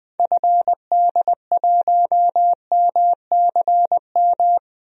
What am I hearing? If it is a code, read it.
Morse code, heard as a keyed tone, about 20 wpm, 700 Hz. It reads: FD1MCM
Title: DXCC-100 Call Signs - 6 Characters - Repeated 3x - 20wpm